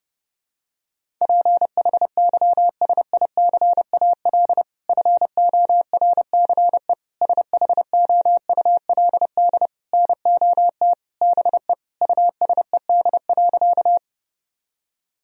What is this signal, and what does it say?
Morse code, heard as a keyed tone, about 30 wpm, 700 Hz. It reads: PHYSICAL FORCE SHOULD NOT BE USED.
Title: Sentences from Top 1000 Words - Review Part 4 - 30wpm